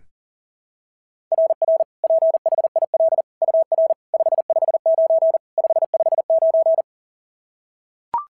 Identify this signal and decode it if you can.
Morse code, heard as a keyed tone, about 37 wpm, 690 Hz. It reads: RR PHIL UR 559 559
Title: SOTA QSO Practice 40wpm